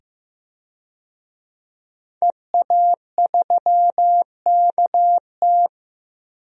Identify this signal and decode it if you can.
Morse code, heard as a keyed tone, about 15 wpm, 700 Hz. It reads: EA3KT
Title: DXCC-100 Call Signs - 5 Characters - 15wpm